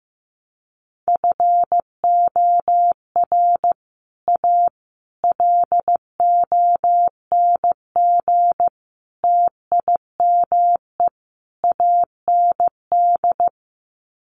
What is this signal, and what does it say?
Morse code, heard as a keyed tone, about 15 wpm, 700 Hz. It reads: FOR A LONG TIME AND
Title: Sets of 5 Words Encore2 15wpm